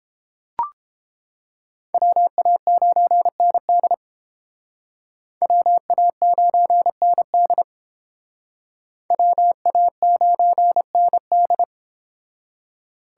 Morse code, heard as a keyed tone, about 29 wpm, 700 Hz. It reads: WA9ND WA9ND WA9ND